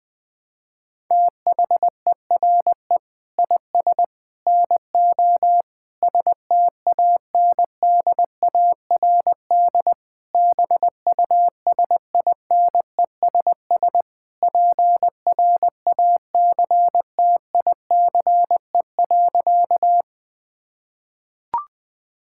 Morse code, heard as a keyed tone, about 20 wpm, 700 Hz. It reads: THERE IS NO STANDARD BUSINESS PRACTICE.
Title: Sentences from Top 1000 Words - Review Part 6 - 20wpm